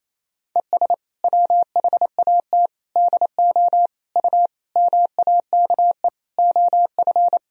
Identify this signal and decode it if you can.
Morse code, heard as a keyed tone, about 28 wpm, 700 Hz. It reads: ES WHAT DO U MAKE OF
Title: Sentences from Top 100 Words and Abbreviations 28wpm